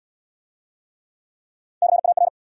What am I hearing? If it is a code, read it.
Morse code, heard as a keyed tone, about 74 wpm, 700 Hz. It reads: CAR